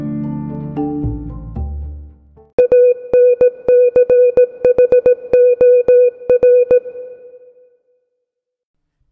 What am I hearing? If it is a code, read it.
Morse code, heard as a keyed tone, about 17 wpm, 500 Hz. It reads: ANCHOR